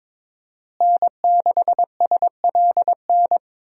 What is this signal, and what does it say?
Morse code, heard as a keyed tone, about 21 wpm, 700 Hz. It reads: N6SLN
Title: USA Call Signs - 5 Characters - Repeated 2x - 22wpm